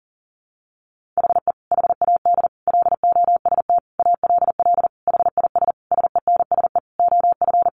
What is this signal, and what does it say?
Morse code, heard as a keyed tone, about 40 wpm, 700 Hz. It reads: HE HAD LOST ALL HIS SENSE OF
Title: Sentences from Top 1400 Words 40wpm